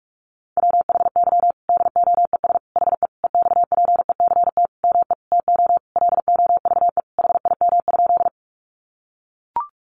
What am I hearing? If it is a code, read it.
Morse code, heard as a keyed tone, about 44 wpm, 700 Hz. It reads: WHY DOES HE EXPECT ME TO LOVE HIM?
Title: Sentences from Top 1100 Words 45wpm